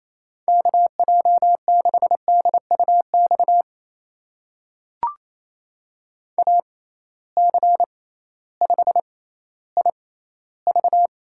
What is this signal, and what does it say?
Morse code, heard as a keyed tone, about 28 wpm, spaced as for 15 wpm, 700 Hz. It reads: KJ6DUX AC5IV
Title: USA Call Signs - 6x Character Spacing - 28wpm